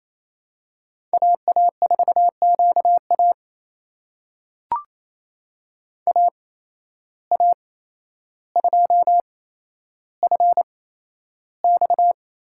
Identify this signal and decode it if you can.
Morse code, heard as a keyed tone, about 27 wpm, spaced as for 12 wpm, 700 Hz. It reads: AA4QA AA2FX
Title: USA Call Signs - 5 Characters - 8x Character Spacing - 28wpm